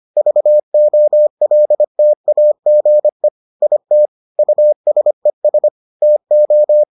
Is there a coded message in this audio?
Yes: Morse code from 0.2 to 6.8 s, heard as a keyed tone, about 25 wpm, 600 Hz. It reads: VOLTAGE IT USES TO